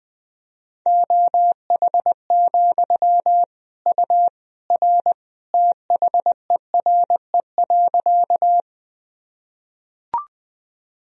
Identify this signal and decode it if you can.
Morse code, heard as a keyed tone, about 20 wpm, 700 Hz. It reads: OH, U R THERE.
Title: Sentences from Top 500 Words and Abbreviations 20wpm